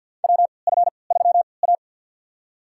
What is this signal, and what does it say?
Morse code, heard as a keyed tone, about 51 wpm, spaced as for 28 wpm, 700 Hz. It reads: WF3A
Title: USA Call Signs - 3x Character Spacing - 50wpm